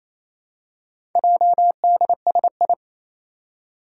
Morse code, heard as a keyed tone, about 27 wpm, 700 Hz. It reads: JDSI
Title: Four Challenging Letter-Number Combinations 28wpm